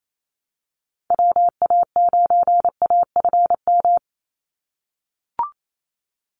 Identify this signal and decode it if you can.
Morse code, heard as a keyed tone, about 28 wpm, 700 Hz. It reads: WA9AFM
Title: USA Call Signs 28wpm